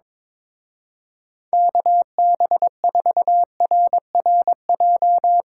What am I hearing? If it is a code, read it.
Morse code, heard as a keyed tone, about 22 wpm, 700 Hz. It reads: KB4RRJ
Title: Speed Racing - USA 2x3 Call Signs 22wpm